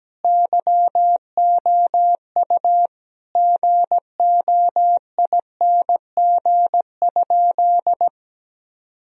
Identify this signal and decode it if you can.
Morse code, heard as a keyed tone, about 17 wpm, 700 Hz. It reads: YOU GOING?
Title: Sentences from Top 1000 Words - Review Part 2 - 17wpm